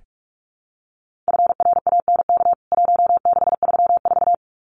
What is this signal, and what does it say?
Morse code, heard as a keyed tone, about 44 wpm, 700 Hz. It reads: FRANK 1634
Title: CWT Exchange 45wpm